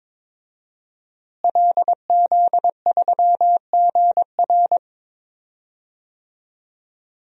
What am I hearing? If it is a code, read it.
Morse code, heard as a keyed tone, about 21 wpm, 700 Hz. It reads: LZ3GR
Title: DXCC-100 Call Signs - 5 Characters - Repeated 3x - 22wpm